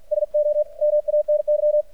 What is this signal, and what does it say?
Morse code, heard as a keyed tone, about 52 wpm, 600 Hz. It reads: SO MANY
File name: archived / CRSnet